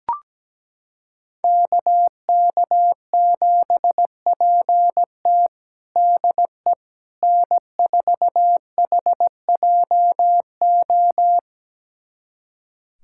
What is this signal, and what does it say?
Morse code, heard as a keyed tone, about 17 wpm, 700 Hz. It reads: KK7PT DE N4HJO
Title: USA Callsign DE Callsign - 5 Characters - 17wpm